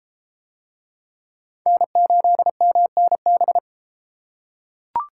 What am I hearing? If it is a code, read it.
Morse code, heard as a keyed tone, about 31 wpm, 700 Hz. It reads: N8MNB